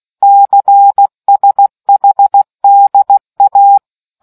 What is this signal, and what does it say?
Morse code, heard as a keyed tone, about 16 wpm, 800 Hz. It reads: CSHDA